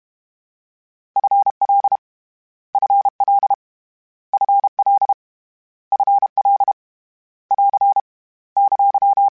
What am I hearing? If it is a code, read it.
Morse code, heard as a keyed tone, about 32 wpm, 800 Hz. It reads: FL FL FL FL <AR> !
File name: Short Runs / Short Runs-mp3s / ShortContrasts @30